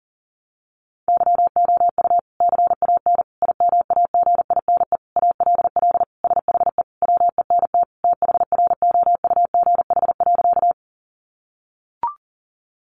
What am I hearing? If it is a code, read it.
Morse code, heard as a keyed tone, about 40 wpm, 700 Hz. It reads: YOU CAN IMAGINE ALL SHE WENT THROUGH.
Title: Sentences from Top 1400 Words 40wpm